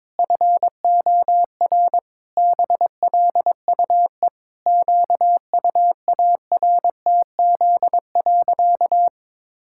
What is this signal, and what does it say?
Morse code, heard as a keyed tone, about 22 wpm, 700 Hz. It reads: FOR BLUE QUARTZ.
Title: Pangram Challenge v2 - 22wpm